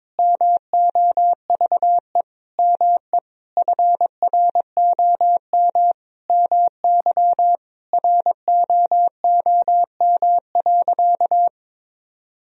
Morse code, heard as a keyed tone, about 22 wpm, 700 Hz. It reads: MOVE ME FROM MY ROOM.
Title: Sentences from Top 1000 Words - Review Part 1 - 22wpm